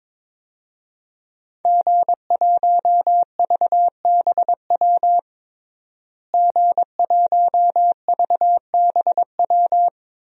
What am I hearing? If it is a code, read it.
Morse code, heard as a keyed tone, about 22 wpm, 700 Hz. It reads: G1VBW G1VBW
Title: DXCC-100 Call Signs - 5 Characters - Repeated 2x - 22wpm